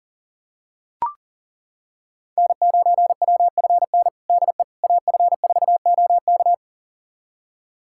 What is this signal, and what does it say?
Morse code, heard as a keyed tone, about 40 wpm, 700 Hz. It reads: N9WFN DE AF4OK
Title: USA Callsign DE Callsign - 5 Characters - 40wpm